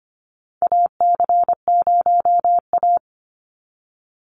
Morse code, heard as a keyed tone, about 25 wpm, 700 Hz. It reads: AC0A
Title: NA SOTA Chaser Callsigns 25wpm